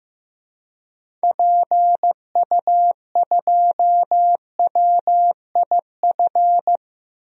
Morse code, heard as a keyed tone, about 16 wpm, 700 Hz. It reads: PU2WIF
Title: DXCC-100 Call Signs - 6 Characters - 15wpm